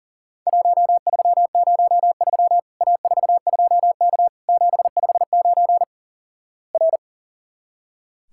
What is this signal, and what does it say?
Morse code, heard as a keyed tone, about 37 wpm, 700 Hz. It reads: 1303 A42K 759 R
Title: Worked All Europe DX Contest - QTC Exchange - 40wpm